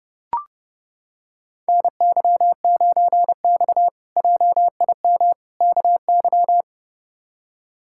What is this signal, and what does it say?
Morse code, heard as a keyed tone, about 31 wpm, 700 Hz. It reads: NY9X JIM KY
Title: K1USN SST QSO Practice - 30wpm